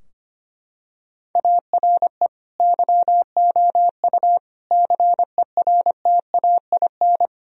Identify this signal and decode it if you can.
Morse code, heard as a keyed tone, about 25 wpm, 700 Hz. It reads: ARE YOU CERTAIN
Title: Sentences from Top 1000 Words - Review Part 2 - 25wpm